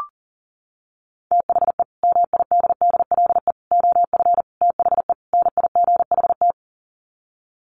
Morse code, heard as a keyed tone, about 40 wpm, 700 Hz. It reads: THE MIDDLE OF THE NIGHT
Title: Sets of 5 Words 40wpm